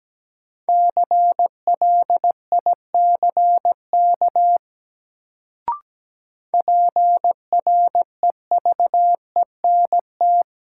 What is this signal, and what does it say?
Morse code, heard as a keyed tone, about 17 wpm, 700 Hz. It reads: CLICK PREVENT
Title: Sentences from Top 1100 Words - New Word Drill Alt - 17wpm